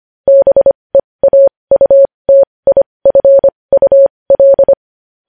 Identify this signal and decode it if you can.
Morse code, heard as a keyed tone, about 25 wpm, spaced as for 20 wpm, 550 Hz. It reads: BEAUTIFUL
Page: News Headlines in Morse Code at 20 WPM